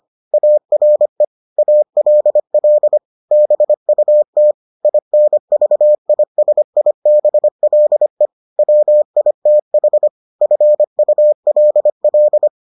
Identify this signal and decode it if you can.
Morse code, heard as a keyed tone, about 25 wpm, 600 Hz. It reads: ARE ALL BUT INVISIBLE WITH FULL